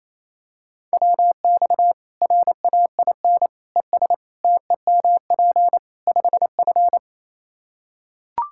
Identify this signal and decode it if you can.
Morse code, heard as a keyed tone, about 27 wpm, 700 Hz. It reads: WX RAIN ES TEMP 5F
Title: K1USN SST QSO Practice 28wpm